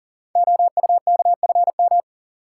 Morse code, heard as a keyed tone, about 39 wpm, 700 Hz. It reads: OUKFM
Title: Code Groups - 5 Letters - 40wpm